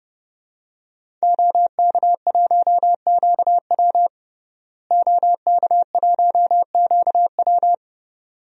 Morse code, heard as a keyed tone, about 31 wpm, 700 Hz. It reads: OK1QW OK1QW
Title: DXCC-100 Call Signs - 5 Characters - Repeated 2x - 30wpm